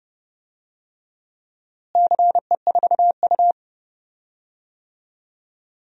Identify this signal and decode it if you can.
Morse code, heard as a keyed tone, about 30 wpm, 700 Hz. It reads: CE4U
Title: DXCC-100 Call Signs - 4 Characters - Repeated 2x - 30wpm